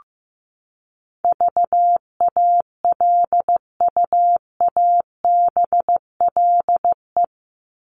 Morse code, heard as a keyed tone, about 15 wpm, 700 Hz. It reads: VALUABLE
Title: Top 1001-1250 Words 15wpm